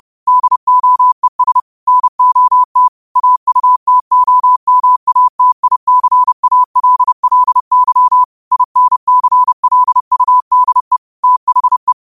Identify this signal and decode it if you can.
Morse code, heard as a keyed tone, about 30 wpm, 1000 Hz. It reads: DOES NOT AUTOMATICALLY INCLUDE THE
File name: qotdfiles / 30wpm